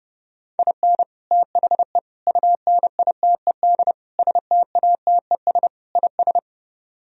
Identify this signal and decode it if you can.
Morse code, heard as a keyed tone, about 32 wpm, 700 Hz. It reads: IN THE UNITED STATES IS